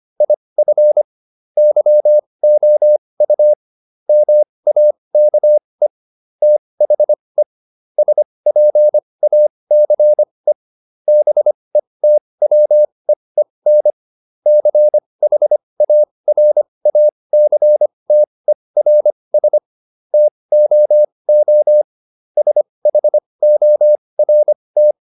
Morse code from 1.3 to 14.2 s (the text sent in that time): YOU MAKE THE SPACE BETWEEN